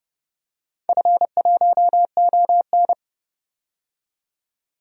Morse code, heard as a keyed tone, about 30 wpm, 700 Hz. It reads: F1ON
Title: DXCC-100 Call Signs - 4 Characters - 30wpm